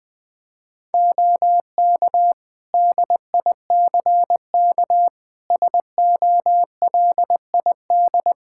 Morse code, heard as a keyed tone, about 21 wpm, 700 Hz. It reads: OK DICK SOLID
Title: Practice QSO - Exchange 3 - More Info - 20wpm